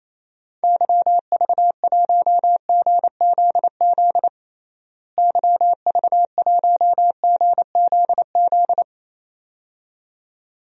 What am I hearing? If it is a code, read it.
Morse code, heard as a keyed tone, about 28 wpm, 700 Hz. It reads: YV1GZZ YV1GZZ
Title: DXCC-100 Call Signs - 6 Characters - Repeated 2x - 28wpm